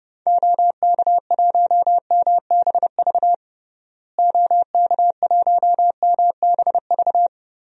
Morse code, heard as a keyed tone, about 30 wpm, 700 Hz. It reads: OK1MBV OK1MBV
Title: DXCC-100 Call Signs - 6 Characters - Repeated 3x - 30wpm